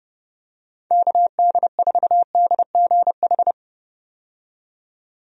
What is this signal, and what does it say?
Morse code, heard as a keyed tone, about 30 wpm, 700 Hz. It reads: KD4DGH
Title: USA Call Signs - 6 Characters - 30wpm